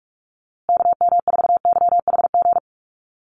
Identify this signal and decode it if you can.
Morse code, heard as a keyed tone, about 44 wpm, 700 Hz. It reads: KM4YHG
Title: USA Call Signs Repeated 2x 45wpm